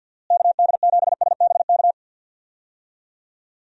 Morse code, heard as a keyed tone, about 52 wpm, 700 Hz. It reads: KD7SBK
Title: USA Call Signs - 6 Characters - Repeated 3x - 50wpm